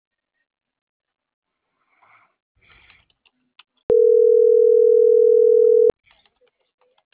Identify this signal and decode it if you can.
Ringback tone (North America), 440 + 480 Hz.